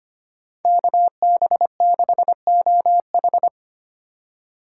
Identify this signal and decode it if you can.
Morse code, heard as a keyed tone, about 32 wpm, 700 Hz. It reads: KB6OH